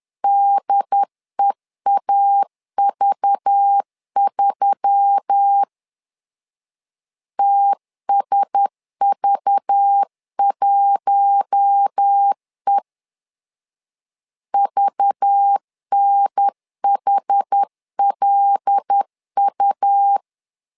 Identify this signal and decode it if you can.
Morse code, heard as a keyed tone, about 10 wpm, 800 Hz. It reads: DEAV3 TSV1E VNHLU
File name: LEZIONI / 5 LBVC / 5 LBV 10wpm 2248